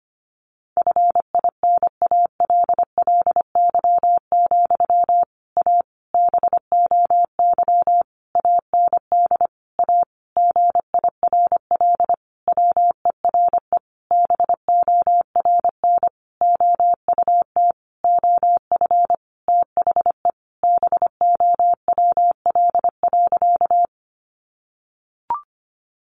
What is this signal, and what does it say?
Morse code, heard as a keyed tone, about 25 wpm, 700 Hz. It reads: FINALLY, A BOY AND A GIRL WERE BORN OUT OF THE BOWL.
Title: Sentences from Top 1200 Words 25wpm